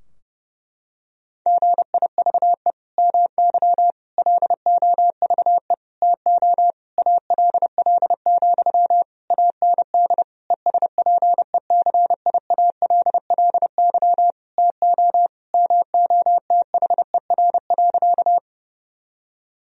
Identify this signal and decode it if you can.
Morse code, heard as a keyed tone, about 30 wpm, 700 Hz. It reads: GIVE MY LOVE TO ALL, AND ESPECIALLY TO MOTHER.
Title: Sentences from Top 1000 Words - Review Part 4 - 30wpm